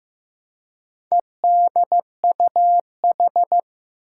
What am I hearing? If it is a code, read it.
Morse code, heard as a keyed tone, about 16 wpm, 700 Hz. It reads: EDUH
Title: Four Challenging Letter-Number Combinations 15wpm